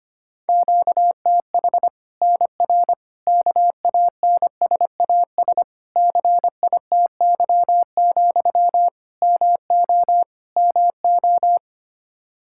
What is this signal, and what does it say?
Morse code, heard as a keyed tone, about 25 wpm, 700 Hz. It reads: QTH NR KANSAS CITY, MO MO
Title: Practice QSO - Exchange 1 - Beginning - 25wpm